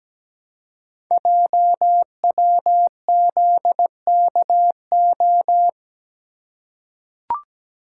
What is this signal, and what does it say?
Morse code, heard as a keyed tone, about 17 wpm, 700 Hz. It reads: JWZKO
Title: Code Groups - 5 Letters - 17wpm